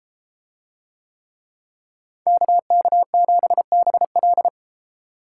Morse code, heard as a keyed tone, about 34 wpm, 700 Hz. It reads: KK7BL